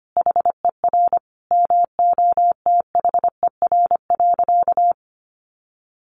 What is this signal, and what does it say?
Morse code, heard as a keyed tone, about 25 wpm, 700 Hz. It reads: HER MOTHER.
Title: Sentences from Top 1500 Words 25wpm